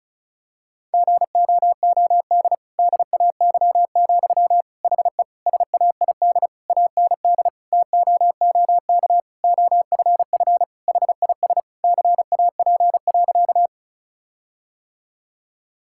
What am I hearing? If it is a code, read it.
Morse code, heard as a keyed tone, about 36 wpm, 700 Hz. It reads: GOOD DAY, HE SAID AND TOOK OFF HIS CAP.
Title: Sentences from Top 1600 Words 35wpm